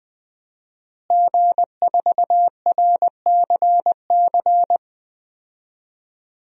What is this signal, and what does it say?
Morse code, heard as a keyed tone, about 20 wpm, 700 Hz. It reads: G4RCC
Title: DXCC-100 Call Signs - 5 Characters - Repeated 2x - 20wpm